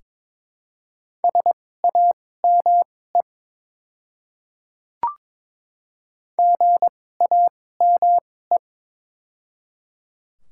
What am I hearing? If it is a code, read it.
Morse code, heard as a keyed tone, about 22 wpm, spaced as for 16 wpm, 700 Hz. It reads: SAME GAME
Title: 2x Character Spacing - Top 500 Words - 22wpm